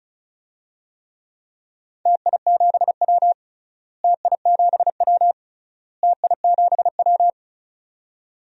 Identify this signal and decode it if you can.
Morse code, heard as a keyed tone, about 34 wpm, 700 Hz. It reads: TI7W TI7W TI7W
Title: DXCC-100 Call Signs - 4 Characters - Repeated 3x - 35wpm